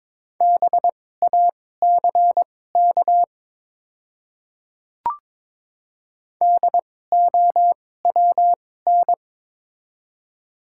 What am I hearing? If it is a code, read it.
Morse code, heard as a keyed tone, about 22 wpm, spaced as for 16 wpm, 700 Hz. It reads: BACK DOWN
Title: 2x Character Spacing - Top 200 Words - 22wpm